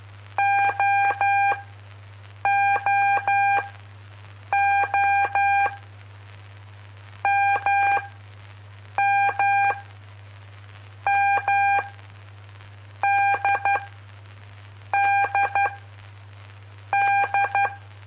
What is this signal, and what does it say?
Morse code, heard as a keyed tone, about 12 wpm, spaced as for 6 wpm, 800 Hz. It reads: OOO MMM DDD
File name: pp5bar / aulas de telegrafia / Aula-05